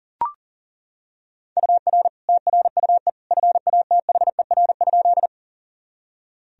Morse code, heard as a keyed tone, about 40 wpm, 700 Hz. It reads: UR TRUE FATHER?
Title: Sentences from Top 400 Words and Abbreviations 40wpm